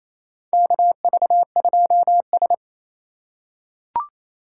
Morse code, heard as a keyed tone, about 28 wpm, 690 Hz. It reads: KV2S
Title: USA Call Signs - 2x Character Spacing - 28wpm